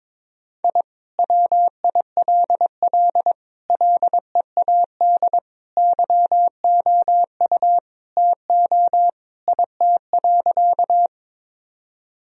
Morse code, heard as a keyed tone, about 22 wpm, 700 Hz. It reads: I WILL LEAD YOU TO IT.
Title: Sentences from Top 1000 Words - Review Part 2 - 22wpm